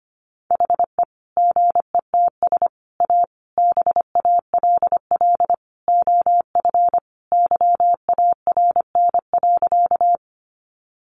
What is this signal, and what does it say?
Morse code, heard as a keyed tone, about 25 wpm, 700 Hz. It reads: HE GETS A BALL OF YARN.
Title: Additional Easy Phrases 25wpm